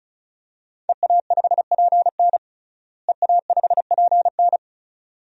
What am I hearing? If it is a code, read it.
Morse code, heard as a keyed tone, about 34 wpm, 700 Hz. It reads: EA5PN EA5PN
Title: DXCC-100 Call Signs - 5 Characters - Repeated 2x - 35wpm